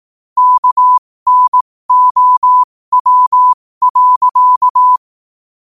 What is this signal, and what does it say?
Morse code, heard as a keyed tone, about 18 wpm, spaced as for 15 wpm, 1000 Hz. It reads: KNOW.
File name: qotdfiles / 15wpm